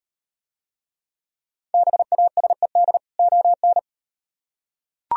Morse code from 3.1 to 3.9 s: ON

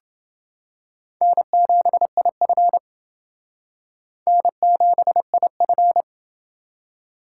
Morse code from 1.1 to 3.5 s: N7IF

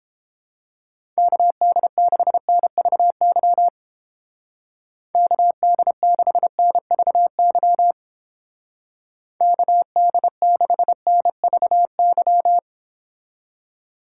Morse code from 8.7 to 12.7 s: KD6NVY

